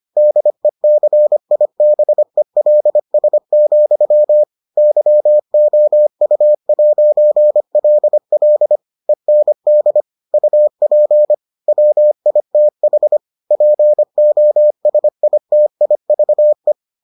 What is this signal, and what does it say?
Morse code, heard as a keyed tone, about 25 wpm, 600 Hz. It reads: DECIBELS, YOU'LL END UP WITH POSITIVE